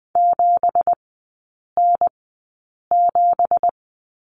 Morse code, heard as a keyed tone, about 20 wpm, spaced as for 8 wpm, 700 Hz. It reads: 7N7
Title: 166 Lesson 27e - Number 7 - Three Characters 20wpm